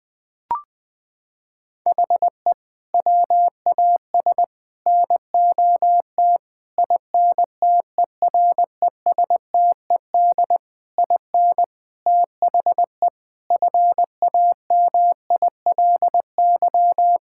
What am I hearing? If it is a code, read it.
Morse code, heard as a keyed tone, about 20 wpm, 700 Hz. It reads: HE WAS NOT INTERESTED IN THE FAMILY
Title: Sentences from Top 1000 Words - Review Part 6 - 20wpm